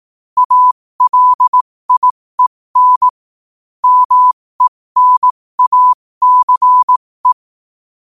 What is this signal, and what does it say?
Morse code, heard as a keyed tone, about 18 wpm, spaced as for 15 wpm, 1000 Hz. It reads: ALIEN MENACE
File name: qotdfiles / 15wpm